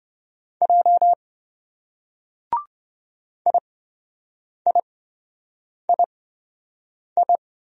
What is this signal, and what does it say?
Morse code, heard as a keyed tone, about 26 wpm, spaced as for 7 wpm, 700 Hz. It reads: J IIII